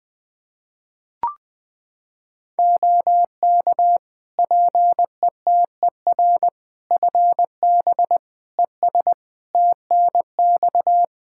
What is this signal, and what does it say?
Morse code, heard as a keyed tone, about 20 wpm, 700 Hz. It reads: OK PETER FB ES TNX
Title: K1USN SST QSO Practice 20wpm